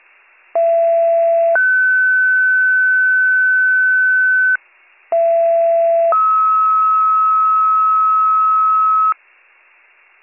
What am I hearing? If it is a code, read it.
Two-tone page, heard as a one-second tone, then a three-second tone, about 670 then 1500 Hz, about 670 then 1290 Hz. It is heard from 0.6 to 9.1 s.